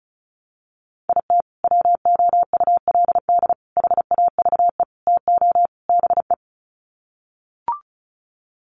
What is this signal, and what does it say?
Morse code, heard as a keyed tone, about 34 wpm, 700 Hz. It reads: IT WOULD HAVE TO BE
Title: Sets of 5 Words Encore 35wpm